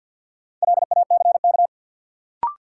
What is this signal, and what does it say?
Morse code, heard as a keyed tone, about 52 wpm, 700 Hz. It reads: LACK
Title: Sentences from Top 1100 Words - New Word Drill Alt - 50wpm